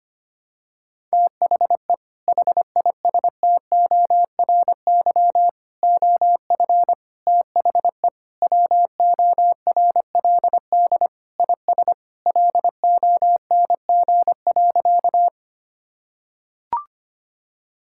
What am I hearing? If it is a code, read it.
Morse code, heard as a keyed tone, about 25 wpm, 700 Hz. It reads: THE HISTORY OF THE WORLD IS LONG.
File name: practice / 25wpm / Sentences with 3-Word Phrases - 25wpm